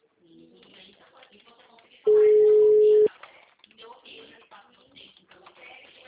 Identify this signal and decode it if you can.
Ringback tone (Europe, 425 Hz), with speech.